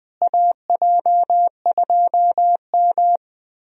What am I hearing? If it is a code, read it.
Morse code, heard as a keyed tone, about 20 wpm, 700 Hz. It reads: AJ2M
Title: USA Call Signs - 4 Characters - Repeated 3x - 20wpm